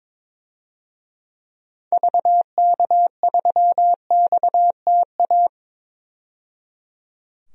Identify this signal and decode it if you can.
Morse code, heard as a keyed tone, about 22 wpm, 700 Hz. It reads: VK3XTA
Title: DXCC-100 Call Signs - 6 Characters - 22wpm